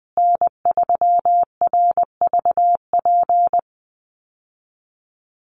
Morse code, heard as a keyed tone, about 20 wpm, 700 Hz. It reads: N3RVP
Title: USA Call Signs Repeated 2x 20wpm